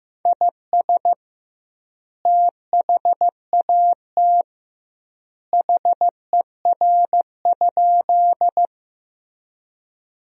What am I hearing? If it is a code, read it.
Morse code, heard as a keyed tone, about 15 wpm, 700 Hz. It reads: IS THAT HER?
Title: 2x Word-Spacing - Sentences from Top 100 Words - 15wpm